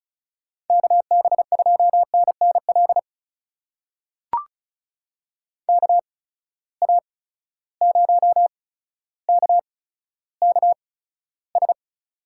Morse code, heard as a keyed tone, about 36 wpm, spaced as for 15 wpm, 700 Hz. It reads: KB2NNL KA0KKS
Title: USA Call Signs - 6 Characters - 8x Character Spacing - 35wpm